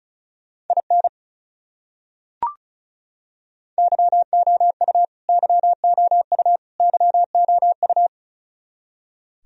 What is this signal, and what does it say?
Morse code, heard as a keyed tone, about 34 wpm, 700 Hz. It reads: IN YOU YOU YOU
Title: Top 100 Words Repeated 35wpm